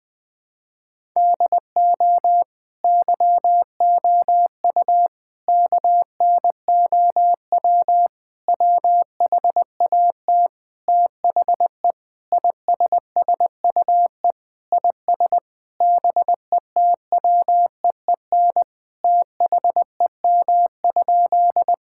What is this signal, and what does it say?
Morse code, heard as a keyed tone, about 21 wpm, 700 Hz. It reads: DO YOU KNOW WHAT THE ISSUE IS BETWEEN THEM?
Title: Sentences from Top 1000 Words - Review Part 6 - 20wpm